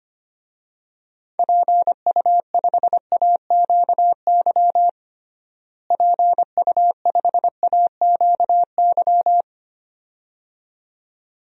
Morse code, heard as a keyed tone, about 25 wpm, 700 Hz. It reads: PU5AQY PU5AQY
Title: DXCC-100 Call Signs - 6 Characters - Repeated 2x - 25wpm